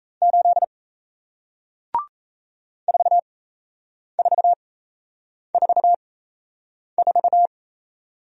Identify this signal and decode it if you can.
Morse code, heard as a keyed tone, about 37 wpm, spaced as for 8 wpm, 700 Hz. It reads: 8 4444